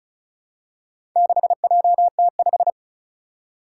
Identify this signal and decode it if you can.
Morse code, heard as a keyed tone, about 36 wpm, 700 Hz. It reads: 6JT5
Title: Four Challenging Letter-Number Combinations 35wpm